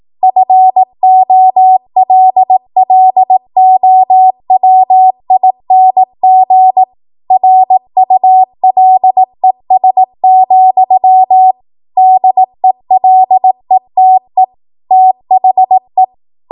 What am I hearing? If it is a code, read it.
Morse code, heard as a keyed tone, about 18 wpm, 750 Hz. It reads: FOLLOWING RULES, DELETE THE
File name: Morse / 18WPM